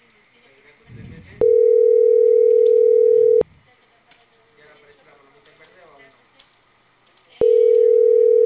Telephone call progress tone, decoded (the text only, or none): ringback tone